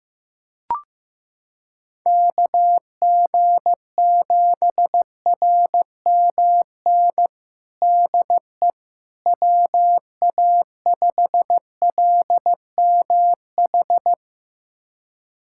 Morse code, heard as a keyed tone, about 16 wpm, 700 Hz. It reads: KG7RMN DE WA5LMH
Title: USA Callsign DE Callsign - 6 Characters - 15wpm